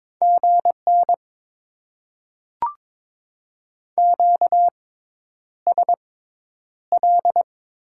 Morse code, heard as a keyed tone, about 22 wpm, spaced as for 10 wpm, 700 Hz. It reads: GN QSL
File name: Top 90 QSO Elements - 6x Character Spacing - 22wpm